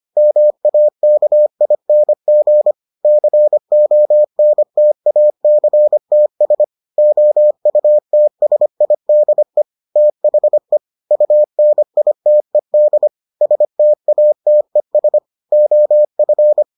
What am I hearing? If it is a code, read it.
Morse code, heard as a keyed tone, about 25 wpm, 600 Hz. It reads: MAKING CONTACTS OUTSIDE THE UNITED STATES OF